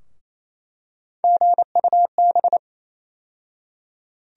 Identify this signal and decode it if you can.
Morse code, heard as a keyed tone, about 28 wpm, 700 Hz. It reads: GUB
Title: USA Call Sign Postfixes - 3 Letters 28wpm